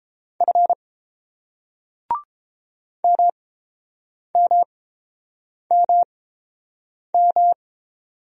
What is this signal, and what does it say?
Morse code, heard as a keyed tone, about 30 wpm, spaced as for 7 wpm, 690 Hz. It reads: F MMMM